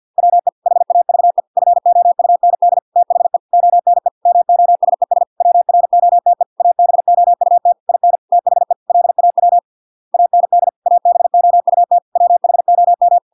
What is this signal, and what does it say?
Morse code, heard as a keyed tone, about 50 wpm, 700 Hz. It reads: WE HAVE FOUND THE ONE MOSES WROTE ABOUT IN THE LAW AND ABOUT WHOM